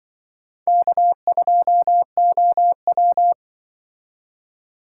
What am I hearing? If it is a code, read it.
Morse code, heard as a keyed tone, about 25 wpm, 700 Hz. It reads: K2OW
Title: Speed Racing - USA 1x2 Call Signs 20wpm